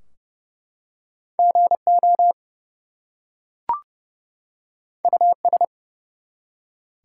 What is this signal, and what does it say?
Morse code, heard as a keyed tone, about 31 wpm, 700 Hz. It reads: GO US
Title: Two Letter Words - 30wpm